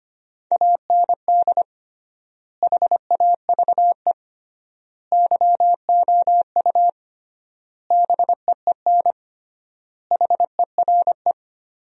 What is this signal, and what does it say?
Morse code, heard as a keyed tone, about 25 wpm, 700 Hz. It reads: AND HAVE YOU BEEN HERE
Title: 3x Word-Spacing - Sentences from Top 200 Words - 25wpm